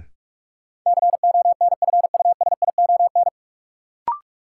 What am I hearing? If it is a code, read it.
Morse code, heard as a keyed tone, about 46 wpm, 700 Hz. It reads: CONFUSION
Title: Sentences from Top 1600 Words - New Word Drill Alt - 45wpm